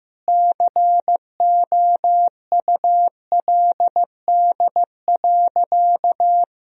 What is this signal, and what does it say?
Morse code, heard as a keyed tone, about 16 wpm, 700 Hz. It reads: COULD.
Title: Sentences from Top 1000 Words - Review Part 2 - 15wpm